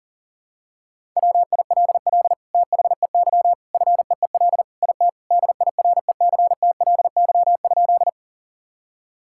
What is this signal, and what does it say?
Morse code, heard as a keyed tone, about 40 wpm, 700 Hz. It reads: WILL THEY FEEL IT DIRECTLY?
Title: Sentences from Top 1000 Words - Review Part 5 - 40wpm